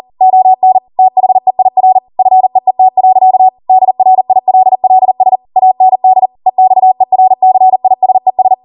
Morse code, heard as a keyed tone, about 40 wpm, 750 Hz. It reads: ON THEIR FEET. DRILLS AND EXERCISES